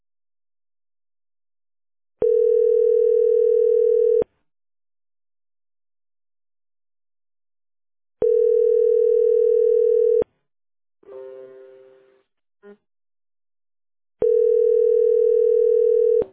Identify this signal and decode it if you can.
Ringback tone (North America), 440 + 480 Hz.